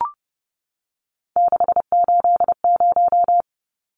Morse code, heard as a keyed tone, about 30 wpm, 700 Hz. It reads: 680
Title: Numbers - 3-Digit 30wpm